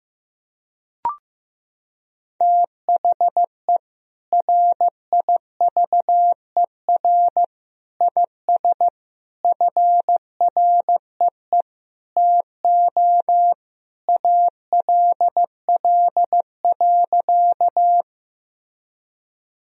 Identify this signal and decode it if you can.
Morse code, heard as a keyed tone, about 16 wpm, 700 Hz. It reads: THE RIVER IS FREE TO ALL.
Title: Sentences from Top 1000 Words - Review Part 1 - 15wpm